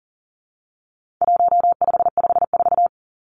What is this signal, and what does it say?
Morse code, heard as a keyed tone, about 37 wpm, 700 Hz. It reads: 1554
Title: Numbers - 4-Digit 40wpm